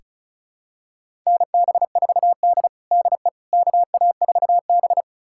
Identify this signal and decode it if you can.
Morse code, heard as a keyed tone, about 36 wpm, 700 Hz. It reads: NB4D DE KA4B
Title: USA Callsign DE Callsign - 4 Characters - 35wpm